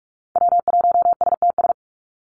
Morse code, heard as a keyed tone, about 44 wpm, 700 Hz. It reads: W1STS
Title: USA Call Signs 45wpm